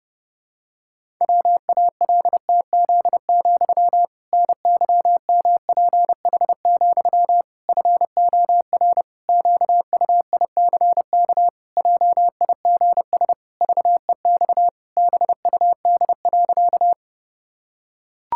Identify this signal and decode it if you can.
Morse code, heard as a keyed tone, about 30 wpm, 700 Hz. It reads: WALTZ, NYMPH, FOR QUICK JIGS VEX BUD.
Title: Pangram Challenge v2 - 30wpm